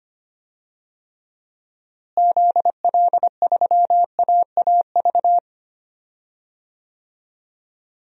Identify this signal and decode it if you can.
Morse code, heard as a keyed tone, about 25 wpm, 700 Hz. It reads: ZL3AAV
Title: DXCC-100 Call Signs - 6 Characters - Repeated 2x - 25wpm